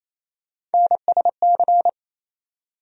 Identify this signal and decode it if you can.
Morse code, heard as a keyed tone, about 28 wpm, 700 Hz. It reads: NSC
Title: USA Call Sign Postfixes - 3 Letters 28wpm